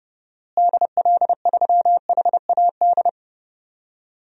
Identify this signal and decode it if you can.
Morse code, heard as a keyed tone, about 31 wpm, 700 Hz. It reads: DL3HAD
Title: DXCC-100 Call Signs - 6 Characters - Repeated 2x - 30wpm